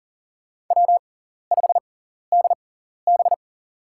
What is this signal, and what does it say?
Morse code, heard as a keyed tone, about 39 wpm, spaced as for 14 wpm, 690 Hz. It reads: W5DB
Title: USA Call Signs - 4 Characters - 6x Character Spacing - 40wpm